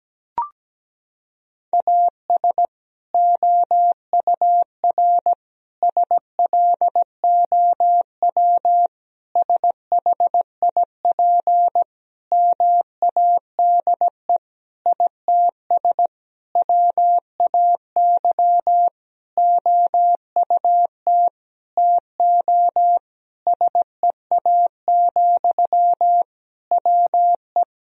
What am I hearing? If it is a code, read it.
Morse code, heard as a keyed tone, about 17 wpm, 700 Hz. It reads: AS OUR SLOW SHIP MADE ITS WAY OUT TO SEA, WE
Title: Sentences from Top 1000 Words - Review Part 2 - 17wpm